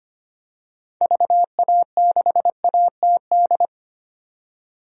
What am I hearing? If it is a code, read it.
Morse code, heard as a keyed tone, about 25 wpm, 700 Hz. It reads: VA6ATD
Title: DXCC-100 Call Signs - 6 Characters - 25wpm